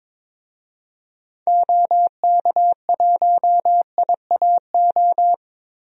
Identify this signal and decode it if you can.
Morse code, heard as a keyed tone, about 22 wpm, 700 Hz. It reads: OK1IAO
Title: DXCC-100 Call Signs - 6 Characters - Repeated 3x - 22wpm